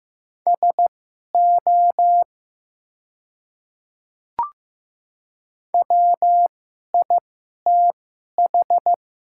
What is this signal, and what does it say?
Morse code, heard as a keyed tone, about 15 wpm, spaced as for 11 wpm, 700 Hz. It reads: SO WITH
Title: 2x Character Spacing - Top 300 Words - 15wpm